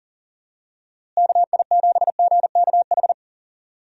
Morse code, heard as a keyed tone, about 39 wpm, 700 Hz. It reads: KI7GKH
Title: USA Call Signs - 6 Characters - Repeated 2x - 40wpm